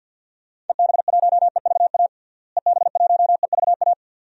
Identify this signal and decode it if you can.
Morse code, heard as a keyed tone, about 50 wpm, 700 Hz. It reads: EB1EVA EB1EVA
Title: DXCC-100 Call Signs - 6 Characters - Repeated 2x - 50wpm